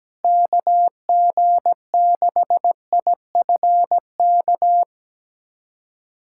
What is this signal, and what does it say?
Morse code, heard as a keyed tone, about 17 wpm, 700 Hz. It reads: KG6IFK
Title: USA Callsign DE Callsign 17wpm